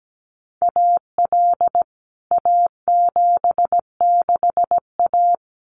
Morse code, heard as a keyed tone, about 17 wpm, 700 Hz. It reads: AL A76A
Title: CWT Exchange 17wpm